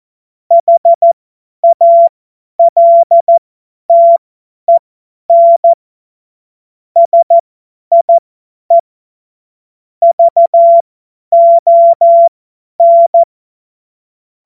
Morse code, heard as a keyed tone, about 14 wpm, spaced as for 10 wpm, 680 Hz. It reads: HALTEN SIE VON
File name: Morsekurs / Witze / Witz 11. Januar 2022, 50 F70